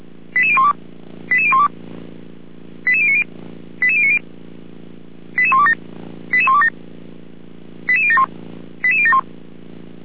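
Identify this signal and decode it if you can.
Five-tone selective call (ZVEI), heard as quick runs of five tones, tones of about 70 ms. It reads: 88012 88012 88090 88090 88127 88127 88071 88071
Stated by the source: ZVEI five-tone call received by SDR.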